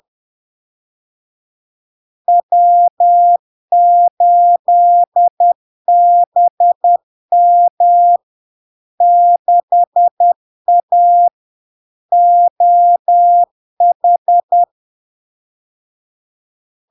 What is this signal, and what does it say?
Morse code, heard as a keyed tone, about 10 wpm, 700 Hz. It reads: W8BM 6A OH
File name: LARK-FD-2022-Log-in-CW-10wpm-700-Hz